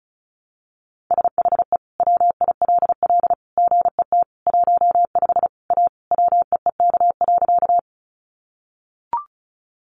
Morse code, heard as a keyed tone, about 34 wpm, 700 Hz. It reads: SHE WILL GET 15 A WEEK.
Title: Easy Phrases 35wpm